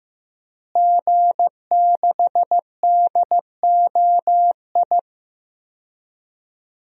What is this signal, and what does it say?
Morse code, heard as a keyed tone, about 15 wpm, 700 Hz. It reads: G6DOI
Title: DXCC-100 Call Signs - 5 Characters - Repeated 3x - 15wpm